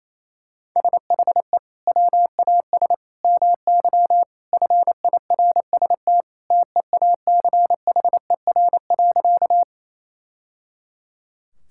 Morse code, heard as a keyed tone, about 28 wpm, 700 Hz. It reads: SHE WAS MY FIRST TEACHER.
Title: Sentences from Top 1000 Words - Review Part 6 - 28wpm